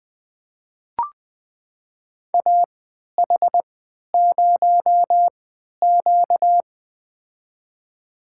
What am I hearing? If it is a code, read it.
Morse code, heard as a keyed tone, about 21 wpm, spaced as for 12 wpm, 700 Hz. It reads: AH0Q
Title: USA Call Signs - 4 Characters - 3x Character Spacing - 20wpm